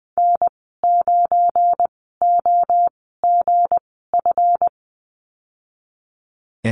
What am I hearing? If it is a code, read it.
Morse code, heard as a keyed tone, about 20 wpm, spaced as for 14 wpm, 700 Hz. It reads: N9OGF
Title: 245 Lesson 39g - Letter z - Call Signs 20wpm